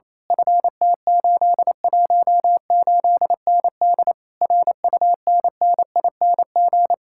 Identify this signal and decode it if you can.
Morse code, heard as a keyed tone, about 26 wpm, 700 Hz. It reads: FT818ND RUNNING
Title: Speed-Racing Common QSO Phrases Encore - 28wpm